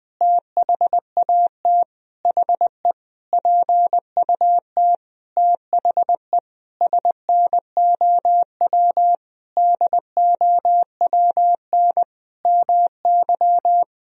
Morse code, heard as a keyed tone, about 21 wpm, 700 Hz. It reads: THAT HE PUT THE SNOW DOWN MY
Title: Sentences from Top 1000 Words - Review Part 3 - 20wpm